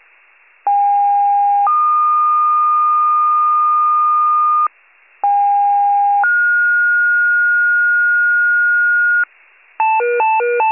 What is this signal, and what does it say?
Two-tone page, heard as a one-second tone, then a three-second tone, about 810 then 1220 Hz, about 810 then 1430 Hz, with noise. It is heard from 0.7 to 9.2 s.